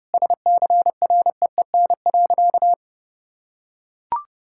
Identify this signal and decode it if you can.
Morse code, heard as a keyed tone, about 30 wpm, 700 Hz. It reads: SCREEN.
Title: Sentences from Top 1000 Words - Review Part 6 - 30wpm